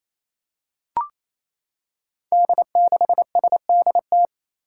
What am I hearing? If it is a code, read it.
Morse code, heard as a keyed tone, about 28 wpm, 700 Hz. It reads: D6SDT
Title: Five Challenging Letter-Number Combinations 28wpm